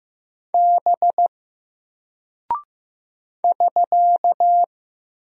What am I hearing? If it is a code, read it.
Morse code, heard as a keyed tone, about 15 wpm, 700 Hz. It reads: B <SK>
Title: Single Character (Letter:Number:Punctuation:Prosign) Alt 15wpm